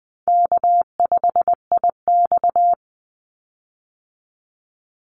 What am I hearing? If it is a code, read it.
Morse code, heard as a keyed tone, about 20 wpm, 700 Hz. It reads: K5IX
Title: Easy Call Signs Repeated 2x - 20wpm